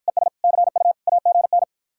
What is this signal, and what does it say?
Morse code, heard as a keyed tone, about 53 wpm, 700 Hz. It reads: ES CU AGN